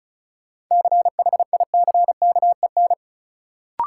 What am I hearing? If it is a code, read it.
Morse code, heard as a keyed tone, about 36 wpm, 700 Hz. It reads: CHICKEN
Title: Sentences from Top 1000 Words - Review Part 5 - New Word Drill - 35wpm